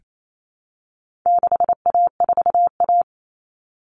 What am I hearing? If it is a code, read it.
Morse code, heard as a keyed tone, about 28 wpm, 700 Hz. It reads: 6A4A
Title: Common Cut Numbers - 4-Digit 28wpm